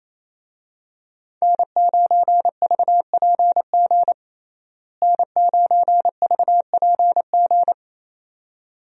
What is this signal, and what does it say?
Morse code, heard as a keyed tone, about 28 wpm, 700 Hz. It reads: N9VPG N9VPG
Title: USA Call Signs - 5 Characters - Repeated 2x - 28wpm